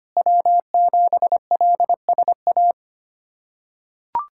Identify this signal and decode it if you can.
Morse code, heard as a keyed tone, about 25 wpm, 700 Hz. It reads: W7LSA
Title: USA Callsign DE Callsign 25wpm